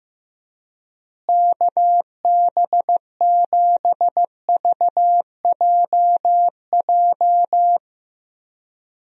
Morse code, heard as a keyed tone, about 14 wpm, 700 Hz. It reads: KB7VJJ
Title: USA Call Signs - 6 Characters - Repeated 3x - 15wpm